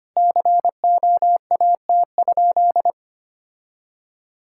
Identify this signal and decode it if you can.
Morse code, heard as a keyed tone, about 25 wpm, 700 Hz. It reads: COAT?
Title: Sentences from Top 1600 Words 25wpm